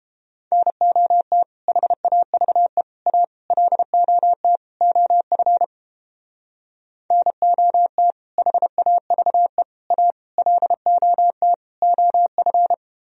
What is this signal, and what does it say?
Morse code, heard as a keyed tone, about 31 wpm, 700 Hz. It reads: NOT HAVE A LOT OF NOT HAVE A LOT OF